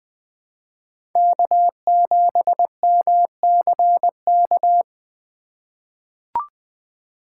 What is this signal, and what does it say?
Morse code, heard as a keyed tone, about 20 wpm, 700 Hz. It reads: K7MCK
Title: USA Call Signs - 5 Characters - 4x Character Spacing - 20wpm